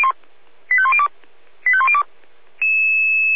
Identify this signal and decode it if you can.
Five-tone selective call (ZVEI), heard as quick runs of five tones, tones of about 70 ms. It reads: 85102 85102